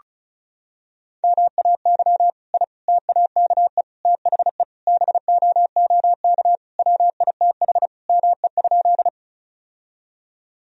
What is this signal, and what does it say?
Morse code, heard as a keyed tone, about 36 wpm, 700 Hz. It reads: MAY I TAKE THE BOOK WITH ME?
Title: Sentences from Top 1000 Words - Review Part 1 - 35wpm